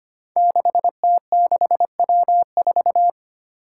Morse code, heard as a keyed tone, about 25 wpm, 700 Hz. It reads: 6T6W4
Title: Five Challenging Letter-Number Combinations 25wpm